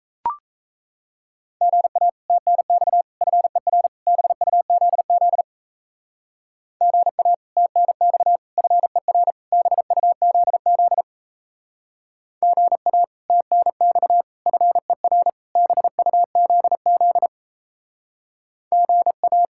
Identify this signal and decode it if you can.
Morse code, heard as a keyed tone, about 34 wpm, 700 Hz. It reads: GA TNX FER BUZZ GA TNX FER BUZZ GA TNX FER BUZZ GA
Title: Speed-Racing Common QSO Phrases - 28wpm